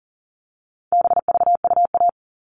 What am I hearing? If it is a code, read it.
Morse code, heard as a keyed tone, about 40 wpm, 700 Hz. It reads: BVUA
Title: All Cut Numbers - 4-Digit 40wpm